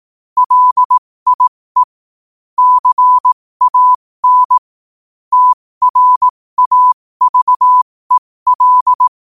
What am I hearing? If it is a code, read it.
Morse code, heard as a keyed tone, about 18 wpm, spaced as for 15 wpm, 1000 Hz. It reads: LIE CAN TRAVEL
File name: qotdfiles / 15wpm